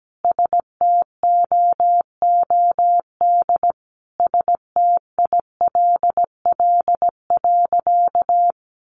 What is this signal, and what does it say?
Morse code, heard as a keyed tone, about 17 wpm, 700 Hz. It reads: STOOD STILL.
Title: Sentences from Top 500 Words 17wpm